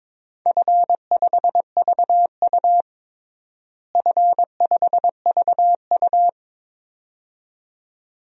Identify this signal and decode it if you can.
Morse code, heard as a keyed tone, about 22 wpm, 700 Hz. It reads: F5VU F5VU
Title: DXCC-100 Call Signs - 4 Characters - Repeated 2x - 22wpm